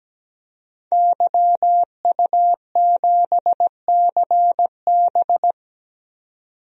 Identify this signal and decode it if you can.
Morse code, heard as a keyed tone, about 17 wpm, 700 Hz. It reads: YU7CB
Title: DXCC-100 Call Signs - 5 Characters - 17wpm